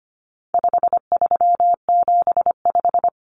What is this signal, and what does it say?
Morse code, heard as a keyed tone, about 25 wpm, 700 Hz. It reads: 5375
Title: Common Cut Numbers - 4-Digit 25wpm